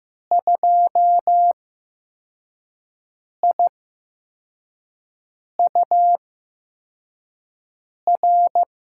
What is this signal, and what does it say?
Morse code, heard as a keyed tone, about 15 wpm, spaced as for 4 wpm, 700 Hz. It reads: 2IUR
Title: USA Call Signs - 6 Characters - 8x Character Spacing - 15wpm